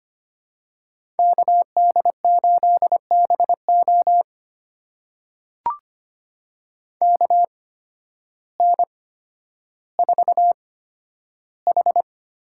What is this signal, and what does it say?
Morse code, heard as a keyed tone, about 25 wpm, spaced as for 12 wpm, 700 Hz. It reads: KD8BO KN4H
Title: USA Call Signs - 8x Character Spacing - 25wpm